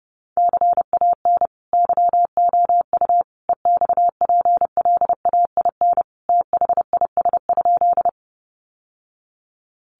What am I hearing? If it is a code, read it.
Morse code, heard as a keyed tone, about 30 wpm, 700 Hz. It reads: CAN YOU EXPLAIN THIS?
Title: Sentences from Top 1100 Words 30wpm